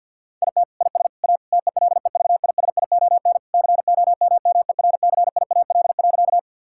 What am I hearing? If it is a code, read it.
Morse code, heard as a keyed tone, about 50 wpm, 700 Hz. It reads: IT IS A TELEVISION COMMERCIAL.
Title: Sentences from Top 1000 Words - Review Part 5 - 50wpm